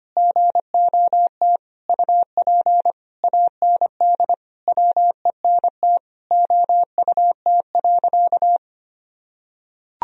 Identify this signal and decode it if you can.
Morse code, heard as a keyed tone, about 25 wpm, 700 Hz. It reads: GOT UP AND WENT OUT.
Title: Sentences from Top 1000 Words - Review Part 1 - 25wpm